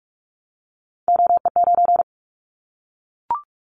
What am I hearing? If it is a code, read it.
Morse code, heard as a keyed tone, about 44 wpm, 700 Hz. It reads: OE9
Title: Callsign Prefixes for Top 100 DXCC - 45wpm